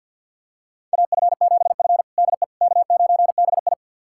Morse code, heard as a keyed tone, about 53 wpm, 700 Hz. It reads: AF7F DE K9BI
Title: USA Callsign DE Callsign - 4 Characters - 50wpm